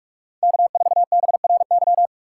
Morse code, heard as a keyed tone, about 46 wpm, 700 Hz. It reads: K4BRY